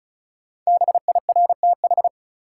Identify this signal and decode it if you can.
Morse code, heard as a keyed tone, about 36 wpm, 700 Hz. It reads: BIRTH
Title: Sentences from Top 800 Words - New Word Drill - 35wpm